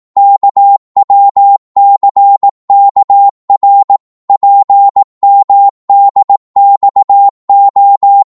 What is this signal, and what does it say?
Morse code, heard as a keyed tone, about 18 wpm, 800 Hz. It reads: KWCKR PMDXO
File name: Morse / ltr-18wpm